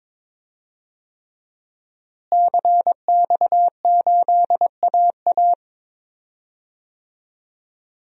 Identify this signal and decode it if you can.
Morse code, heard as a keyed tone, about 22 wpm, 700 Hz. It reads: CX8AA
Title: DXCC-100 Call Signs - 5 Characters - Repeated 3x - 22wpm